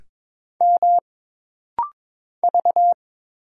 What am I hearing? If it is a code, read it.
Morse code, heard as a keyed tone, about 22 wpm, 700 Hz. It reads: M V
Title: Single Letter-Number Alt - Rapid-Fire 22wpm